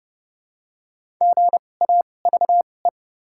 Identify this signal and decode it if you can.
Morse code, heard as a keyed tone, about 30 wpm, spaced as for 22 wpm, 700 Hz. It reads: GAVE
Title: 2x Character Spacing - Top 500 Words - 30wpm